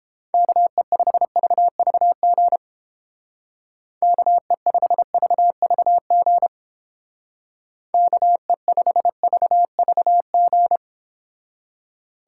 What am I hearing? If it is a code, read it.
Morse code, heard as a keyed tone, about 29 wpm, 700 Hz. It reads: KE5VVG KE5VVG KE5VVG